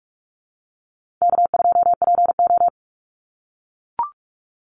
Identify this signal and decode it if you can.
Morse code, heard as a keyed tone, about 44 wpm, 700 Hz. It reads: K2PO
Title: Easy Call Signs - 45wpm